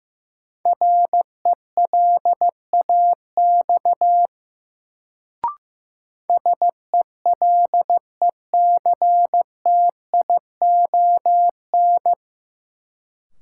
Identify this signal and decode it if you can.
Morse code, heard as a keyed tone, about 15 wpm, 700 Hz. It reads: RELAX SELECTION
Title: Sentences from Top 1400 Words - New Word Drill - 15wpm